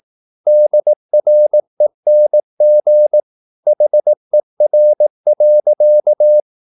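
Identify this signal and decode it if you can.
Morse code, heard as a keyed tone, about 18 wpm, 600 Hz. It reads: DRENG HER.
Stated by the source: Slow (QRS) Morse practice at 18 wpm, CW med Gnister.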